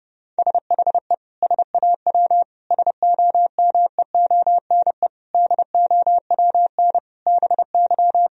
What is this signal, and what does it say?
Morse code, heard as a keyed tone, about 28 wpm, 700 Hz. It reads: SHE SAW SOMEONE DOWN BY
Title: Sentences from Top 1000 Words - Review Part 4 - 30wpm